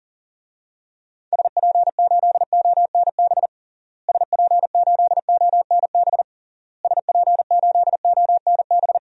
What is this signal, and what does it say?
Morse code, heard as a keyed tone, about 40 wpm, 700 Hz. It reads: SP8ONB SP8ONB SP8ONB
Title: DXCC-100 Call Signs - 6 Characters - Repeated 3x - 40wpm